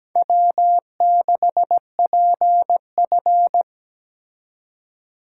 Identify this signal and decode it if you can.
Morse code, heard as a keyed tone, about 20 wpm, 700 Hz. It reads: W6PF
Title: Speed Racing - USA 1x2 Call Signs 17wpm